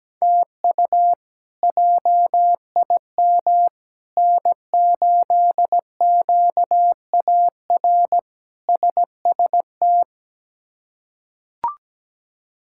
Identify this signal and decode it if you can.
Morse code, heard as a keyed tone, about 17 wpm, 700 Hz. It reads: TU JIM N8QAR SST
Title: K1USN SST QSO Practice - 17wpm